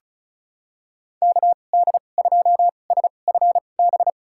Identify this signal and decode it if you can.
Morse code, heard as a keyed tone, about 34 wpm, spaced as for 25 wpm, 700 Hz. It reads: KD2SFB
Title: USA Call Signs - 6 Characters - 2x Character Spacing - 35wpm